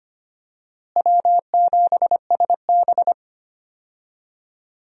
Morse code, heard as a keyed tone, about 25 wpm, 700 Hz. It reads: W7SB
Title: USA Call Signs - 4 Characters - Repeated 2x - 25wpm